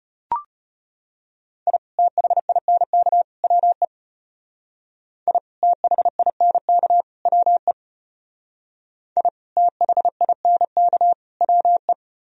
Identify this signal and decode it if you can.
Morse code, heard as a keyed tone, about 32 wpm, 700 Hz. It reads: I THINK WE I THINK WE I THINK WE